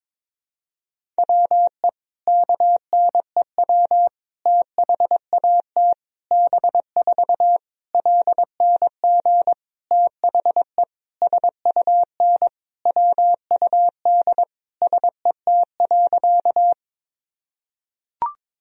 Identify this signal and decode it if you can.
Morse code, heard as a keyed tone, about 22 wpm, 700 Hz. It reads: WE KNEW THAT B4 LNG THE SUN WUD SET.
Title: Sentences from Top 400 Words and Abbreviations 22wpm